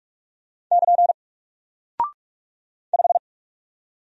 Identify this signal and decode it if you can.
Morse code, heard as a keyed tone, about 44 wpm, 700 Hz. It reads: <KN> 5
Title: Single Character (Letter:Number:Punctuation:Prosign) 45wpm